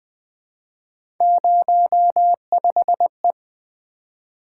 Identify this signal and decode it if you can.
Morse code, heard as a keyed tone, about 20 wpm, 700 Hz. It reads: 05E
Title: Three Letter-Number Combinations - 20wpm